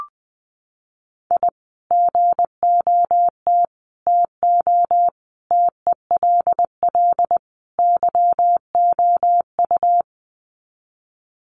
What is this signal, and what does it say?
Morse code, heard as a keyed tone, about 20 wpm, 700 Hz. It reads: I GOT TO TELL YOU
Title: Sets of 5 Words Encore2 20wpm